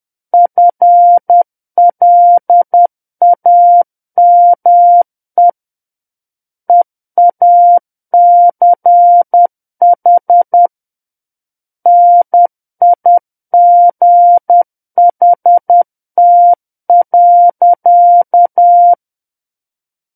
Morse code, heard as a keyed tone, about 10 wpm, 700 Hz. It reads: FLAME EACH NIGHT.
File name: Book One / 01-Chapter 1 - 10 WPM